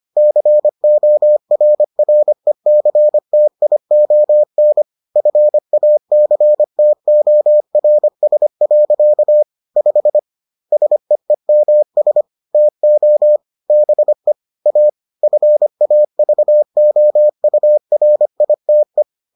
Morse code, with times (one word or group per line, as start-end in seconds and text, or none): 0.2-4.8 CORRECTION
5.2-9.4 FACTORS.
9.8-10.2 5
10.7-12.2 SEEMS
12.5-13.4 TO
13.7-14.3 BE
14.7-14.9 A
15.2-19.0 FAVOURITE